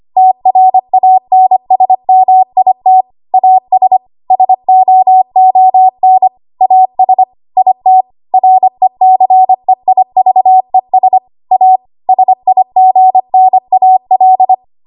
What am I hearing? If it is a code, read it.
Morse code, heard as a keyed tone, about 25 wpm, 750 Hz. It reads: TRANSMIT AS SOON AS IT RECEIVES A SIGNAL
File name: Archive / 25 WPM / 25WPM